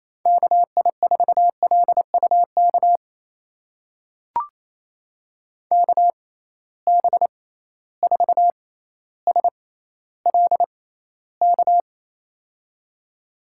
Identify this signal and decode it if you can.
Morse code, heard as a keyed tone, about 27 wpm, spaced as for 14 wpm, 700 Hz. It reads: KI4LUK KB4SLK
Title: USA Call Signs - 6 Characters - 6x Character Spacing - 28wpm